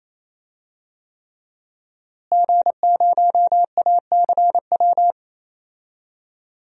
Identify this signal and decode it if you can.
Morse code, heard as a keyed tone, about 28 wpm, 700 Hz. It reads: G0ACW
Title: DXCC-100 Call Signs - 5 Characters - Repeated 3x - 28wpm